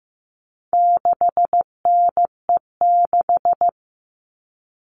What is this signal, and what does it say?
Morse code, heard as a keyed tone, about 15 wpm, 700 Hz. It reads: 6NE6
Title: All Cut Numbers - 4-Digit 15wpm